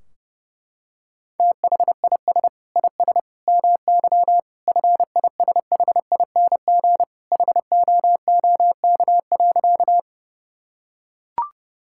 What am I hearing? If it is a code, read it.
Morse code, heard as a keyed tone, about 32 wpm, 700 Hz. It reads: THIS IS MY FISHING HOOK.
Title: Sentences from Top 1600 Words 30wpm